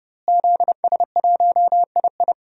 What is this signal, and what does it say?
Morse code, heard as a keyed tone, about 30 wpm, 700 Hz. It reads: ZS1II
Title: DXCC-100 Call Signs - 5 Characters - Repeated 3x - 30wpm